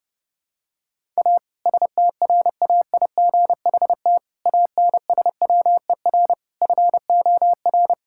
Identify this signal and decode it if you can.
Morse code, heard as a keyed tone, about 30 wpm, 700 Hz. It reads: A STRAIGHT ANSWER FOR
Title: Sentences from Top 1000 Words - Review Part 6 - 30wpm